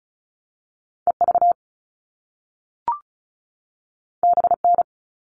Morse code, heard as a keyed tone, about 36 wpm, 700 Hz. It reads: EV BN
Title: All Cut Numbers - 2-Digit 35wpm